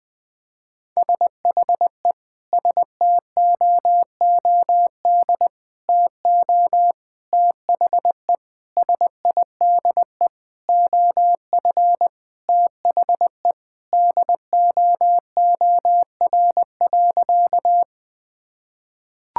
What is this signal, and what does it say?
Morse code, heard as a keyed tone, about 20 wpm, 700 Hz. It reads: SHE STOOD TO THE SIDE OF THE DOOR.
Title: Sentences with 5-Word Phrases - Spoken Beginning - 20wpm